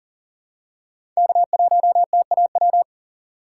Morse code, heard as a keyed tone, about 37 wpm, 700 Hz. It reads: K1TAW
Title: USA Call Signs - 5 Characters - 3x Character Spacing - 40wpm